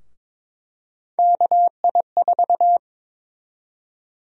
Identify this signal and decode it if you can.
Morse code, heard as a keyed tone, about 22 wpm, 700 Hz. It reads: KI4
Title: USA Call Sign Prefixes - 22wpm